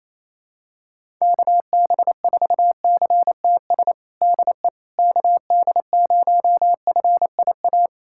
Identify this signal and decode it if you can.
Morse code, heard as a keyed tone, about 28 wpm, 700 Hz. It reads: KB4CTS DE KD0FIA
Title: USA Callsign DE Callsign 28wpm